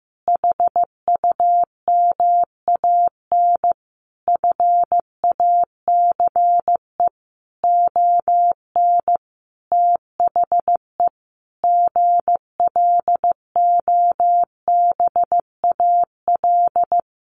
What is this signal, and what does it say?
Morse code, heard as a keyed tone, about 15 wpm, 700 Hz. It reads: HUMAN FACE ON THE GLOBAL
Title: Sentences from Top 1200 Words 15wpm